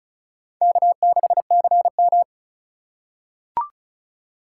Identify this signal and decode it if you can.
Morse code, heard as a keyed tone, about 34 wpm, 700 Hz. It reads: K6CM
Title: USA Call Signs - 4 Characters - 3x Character Spacing - 35wpm